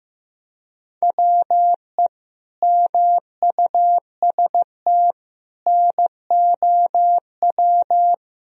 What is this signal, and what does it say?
Morse code, heard as a keyed tone, about 15 wpm, 700 Hz. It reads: WE MUST NOW
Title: Sentences from Top 1000 Words - Review Part 4 - 15wpm